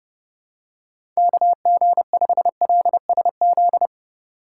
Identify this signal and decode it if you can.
Morse code, heard as a keyed tone, about 30 wpm, 700 Hz. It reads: KG5LSZ